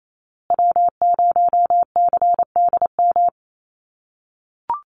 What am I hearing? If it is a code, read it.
Morse code, heard as a keyed tone, about 28 wpm, 700 Hz. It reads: W0CDM
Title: USA Call Signs Repeated 3x 28wpm